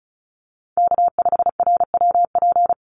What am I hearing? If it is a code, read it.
Morse code, heard as a keyed tone, about 34 wpm, 700 Hz. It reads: K5RWP
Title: NA SOTA Chaser Callsigns 35wpm